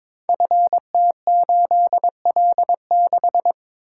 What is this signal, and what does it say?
Morse code, heard as a keyed tone, about 22 wpm, 700 Hz. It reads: FT8L6
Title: Code Groups - 5 Letter:Numbers - 22wpm